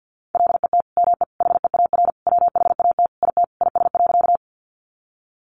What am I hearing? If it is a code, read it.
Morse code, heard as a keyed tone, about 50 wpm, 700 Hz. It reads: LET ME HEAR WHAT IT IS.
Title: Sentences from Top 300 Words 50wpm